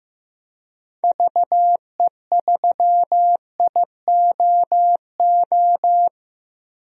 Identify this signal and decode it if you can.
Morse code, heard as a keyed tone, about 15 wpm, 700 Hz. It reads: VE3IOO
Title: DXCC-100 Call Signs - 6 Characters - 15wpm